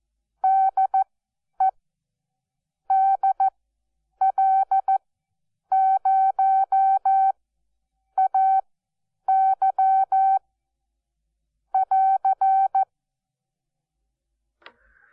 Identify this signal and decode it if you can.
Morse code, heard as a keyed tone, about 14 wpm, spaced as for 8 wpm, 770 Hz. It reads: DE DL0AY <AR>
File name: Onlinelehrgang / morsen / DJ4UF Morsen12